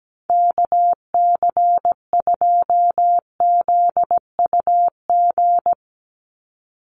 Morse code, heard as a keyed tone, about 17 wpm, 700 Hz. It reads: KC2ZUG
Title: USA Call Signs 17wpm